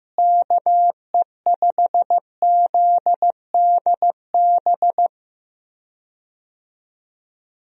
Morse code, heard as a keyed tone, about 15 wpm, 700 Hz. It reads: KE5ZDB
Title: USA Call Signs - 6 Characters - Repeated 2x - 15wpm